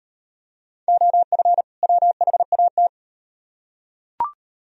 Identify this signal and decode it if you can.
Morse code, heard as a keyed tone, about 38 wpm, 700 Hz. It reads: OF WHAT